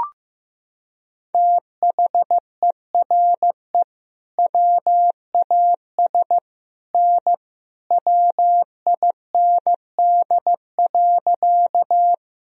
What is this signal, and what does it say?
Morse code, heard as a keyed tone, about 16 wpm, 700 Hz. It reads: THERE WAS N WIND.
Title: Sentences from Top 400 Words and Abbreviations 15wpm